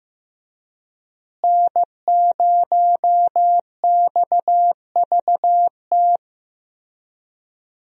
Morse code, heard as a keyed tone, about 15 wpm, 700 Hz. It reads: N0XVT
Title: USA Call Signs - 5 Characters - Repeated 2x - 15wpm